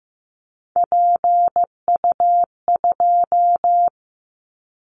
Morse code, heard as a keyed tone, about 15 wpm, 700 Hz. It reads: PU2
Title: Callsign Prefixes for Top 100 DXCC - 15wpm